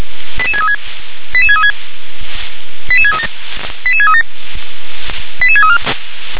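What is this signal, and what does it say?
Five-tone selective call (ZVEI), heard as quick runs of five tones, tones of about 70 ms. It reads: 88636 88636 88637 88637 88634
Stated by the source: ZVEI five-tone call 88510 received by SDR.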